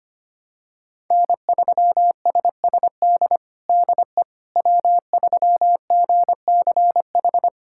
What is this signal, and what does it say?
Morse code, heard as a keyed tone, about 25 wpm, 700 Hz. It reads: N3SSD DE W3GCH
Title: USA Callsign DE Callsign - 5 Characters - 25wpm